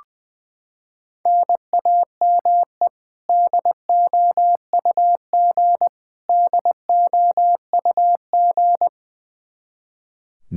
Morse code, heard as a keyed tone, about 21 wpm, 700 Hz. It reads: NAME DOUG DOUG
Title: Practice QSO - Exchange 1 - Beginning - 20wpm